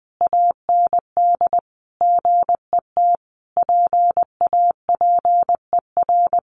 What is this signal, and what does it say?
Morse code, heard as a keyed tone, about 20 wpm, 700 Hz. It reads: AND GET PAPER
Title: Sentences from Top 1500 Words 20wpm